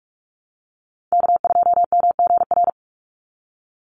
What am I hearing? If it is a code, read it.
Morse code, heard as a keyed tone, about 44 wpm, 700 Hz. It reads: K2MGR
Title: Call Signs Repeated 3x - 45wpm